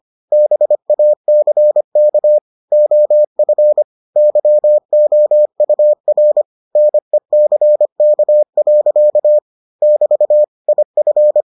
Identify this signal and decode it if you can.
Morse code, heard as a keyed tone, about 25 wpm, 600 Hz. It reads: BACK OF YOUR NECK. <BT> IF